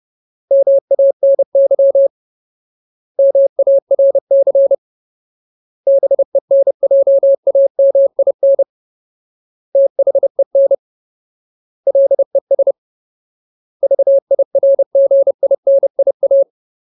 Morse code, heard as a keyed tone, about 30 wpm, 550 Hz. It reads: MANY MARC BENJAMIN THEN LES VIRGINIA
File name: Ultimate Word List / 30WPM Ultimate